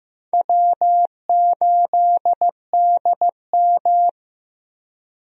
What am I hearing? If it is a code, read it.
Morse code, heard as a keyed tone, about 15 wpm, 700 Hz. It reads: W8DM
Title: USA Call Signs - 4 Characters - Repeated 3x - 15wpm